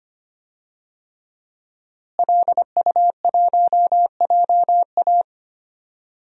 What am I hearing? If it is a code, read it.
Morse code, heard as a keyed tone, about 25 wpm, 700 Hz. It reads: LU1JA
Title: DXCC-100 Call Signs - 5 Characters - Repeated 3x - 25wpm